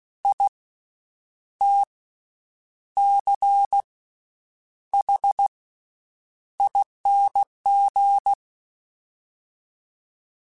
Morse code, heard as a keyed tone, about 16 wpm, spaced as for 8 wpm, 800 Hz. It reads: ITCHING
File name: Special INGs / Special INGs-mp3s / VerySlowToFaster Ing @15